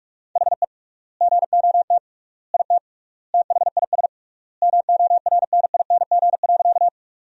Morse code, heard as a keyed tone, about 46 wpm, 700 Hz. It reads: HE GOT IT THIS MORNING.
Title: 3x Word-Spacing - Sentences from Top 400 Words - 45wpm